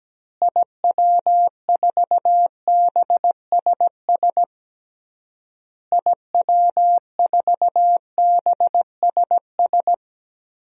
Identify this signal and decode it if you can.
Morse code, heard as a keyed tone, about 17 wpm, 700 Hz. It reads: IW4BSS IW4BSS
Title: DXCC-100 Call Signs - 6 Characters - Repeated 3x - 17wpm